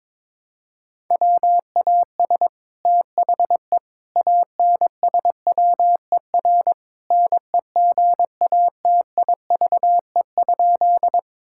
Morse code, heard as a keyed tone, about 21 wpm, 700 Hz. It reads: WAS THE ANSWER NEGATIVE?
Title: Sentences from Top 1000 Words - Review Part 6 - 22wpm